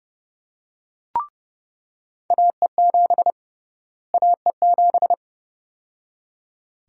Morse code, heard as a keyed tone, about 30 wpm, 700 Hz. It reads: AE7 AE7
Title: USA Call Sign Prefixes - Repeated 2x - 30wpm